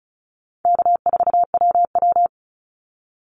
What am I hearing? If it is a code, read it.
Morse code, heard as a keyed tone, about 34 wpm, 700 Hz. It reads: K4WW
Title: Easy Call Signs Repeated 2x - 35wpm